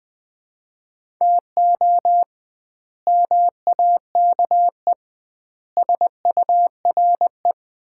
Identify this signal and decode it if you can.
Morse code, heard as a keyed tone, about 21 wpm, 700 Hz. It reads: TO MAKE SURE
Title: 2x Word-Spacing - Sets of 4 Words - 20wpm